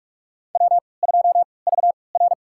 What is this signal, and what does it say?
Morse code, heard as a keyed tone, about 43 wpm, spaced as for 25 wpm, 700 Hz. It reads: W2VR
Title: USA Call Signs - 4 Characters - 3x Character Spacing - 45wpm